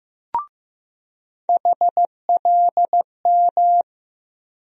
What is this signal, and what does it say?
Morse code, heard as a keyed tone, about 15 wpm, 700 Hz. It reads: HLM
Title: USA Call Sign Postfixes - 3 Letters 15wpm